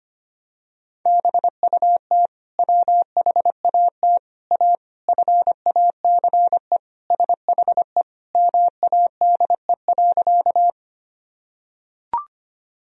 Morse code, heard as a keyed tone, about 25 wpm, 700 Hz. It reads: BUT WHAT A FACE SHE MADE.
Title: Sentences from Top 1000 Words - Review Part 2 - 25wpm